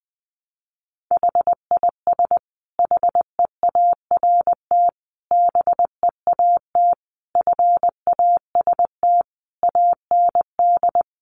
Morse code, heard as a keyed tone, about 20 wpm, 700 Hz. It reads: HIS HEART BEAT FAST AND
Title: Sentences from Top 1300 Words 20wpm